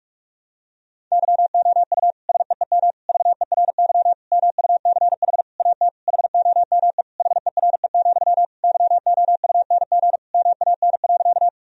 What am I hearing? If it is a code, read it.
Morse code, heard as a keyed tone, about 41 wpm, 700 Hz. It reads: YOU SEEM VERY MUCH AT HOME HERE, YOUNG MAN.
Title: Sentences from Top 1000 Words - Review Part 1 - 45wpm